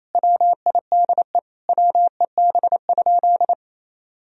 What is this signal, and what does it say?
Morse code, heard as a keyed tone, about 28 wpm, 700 Hz. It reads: WIDE WEB?
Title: Sentences from Top 1000 Words - Review Part 4 - 28wpm